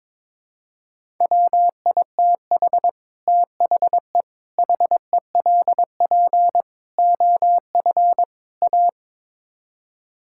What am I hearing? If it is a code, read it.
Morse code, heard as a keyed tone, about 22 wpm, 700 Hz. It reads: WITH THE HELP OF A
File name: practice / 22wpm / Build A Phrase - 5-Word Phrases - 22wpm